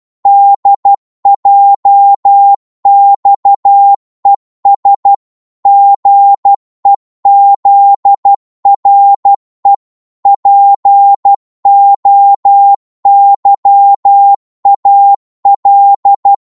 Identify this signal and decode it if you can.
Morse code, heard as a keyed tone, about 12 wpm, 800 Hz. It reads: DJXES GEZRE POYAL
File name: Morse / ltr-12wpm